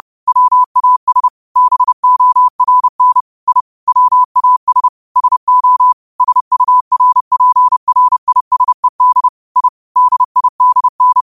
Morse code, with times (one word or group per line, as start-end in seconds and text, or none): 0.3-1.3 WAS
1.6-3.2 BORN
3.5-3.6 I
3.9-4.9 WAS
5.2-5.9 SO
6.2-9.3 SURPRISED
9.6-9.7 I
10.0-11.2 DIDN